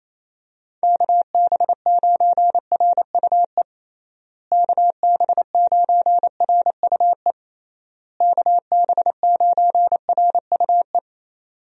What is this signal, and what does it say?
Morse code, heard as a keyed tone, about 28 wpm, 700 Hz. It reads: KB9RUE KB9RUE KB9RUE
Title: USA Call Signs - 6 Characters - Repeated 3x - 28wpm